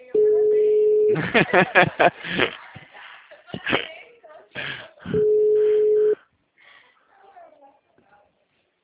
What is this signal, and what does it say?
Ringback tone (Europe, 425 Hz), with speech.
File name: out- -2007- - - .14016